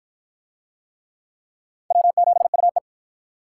Morse code, heard as a keyed tone, about 50 wpm, 700 Hz. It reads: W7UE